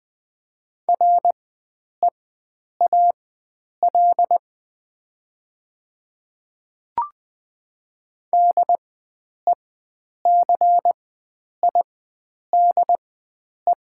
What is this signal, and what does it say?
Morse code, heard as a keyed tone, about 20 wpm, spaced as for 9 wpm, 700 Hz. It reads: REAL DECIDE
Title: 4x Character Spacing - Top 500 Words - 20wpm